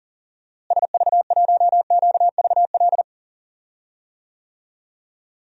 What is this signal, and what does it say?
Morse code, heard as a keyed tone, about 40 wpm, 700 Hz. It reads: SV1QVL
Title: DXCC-100 Call Signs - 6 Characters - Repeated 2x - 40wpm